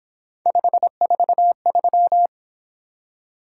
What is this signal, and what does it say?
Morse code, heard as a keyed tone, about 26 wpm, 700 Hz. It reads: 543
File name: Speed-Racing - Three Letter-Number Combinations - 22wpm